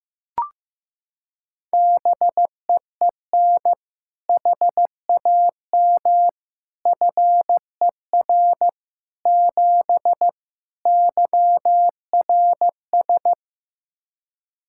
Morse code, heard as a keyed tone, about 15 wpm, 700 Hz. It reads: BEEN HAM FER 7 YRS
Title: K1USN SST QSO Practice 15wpm